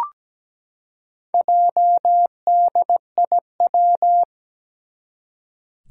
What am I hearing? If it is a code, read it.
Morse code, heard as a keyed tone, about 17 wpm, 700 Hz. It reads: JDIW
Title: Four Challenging Letter-Number Combinations 17wpm